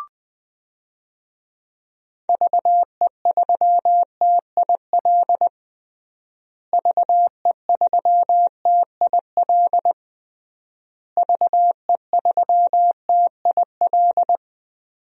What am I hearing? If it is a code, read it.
Morse code, heard as a keyed tone, about 20 wpm, 700 Hz. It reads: VE3TIL VE3TIL VE3TIL
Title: DXCC-100 Call Signs - 6 Characters - Repeated 3x - 20wpm